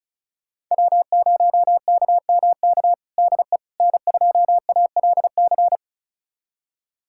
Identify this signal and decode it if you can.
Morse code, heard as a keyed tone, about 36 wpm, 700 Hz. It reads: W0KMK DE N2ALC
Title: USA Callsign DE Callsign - 5 Characters - 35wpm